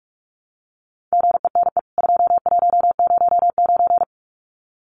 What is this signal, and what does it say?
Morse code, heard as a keyed tone, about 44 wpm, 700 Hz. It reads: GENE 2109
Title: CWT Exchange 45wpm